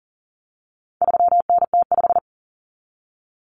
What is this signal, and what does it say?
Morse code, heard as a keyed tone, about 40 wpm, 700 Hz. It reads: 3NT5
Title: Common Cut Numbers - 4-Digit 40wpm